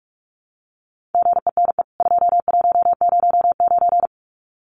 Morse code, heard as a keyed tone, about 44 wpm, 700 Hz. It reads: GENE 2109
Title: CWT Exchange 45wpm